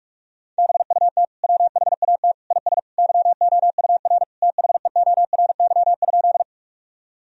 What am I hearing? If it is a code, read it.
Morse code, heard as a keyed tone, about 41 wpm, 700 Hz. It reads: BUT WHAT IS YOUR THEORY?
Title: Sentences from Top 1000 Words - Review Part 3 - 45wpm